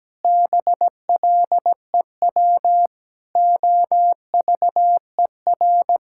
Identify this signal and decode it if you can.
Morse code, heard as a keyed tone, about 17 wpm, 700 Hz. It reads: BLEW OVER
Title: Pangram Challenge v2 - 17wpm